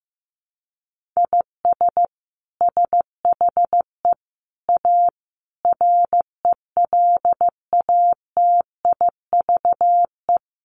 Morse code, heard as a keyed tone, about 15 wpm, 700 Hz. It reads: IS SHE A RELATIVE
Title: Sentences from Top 1400 Words 15wpm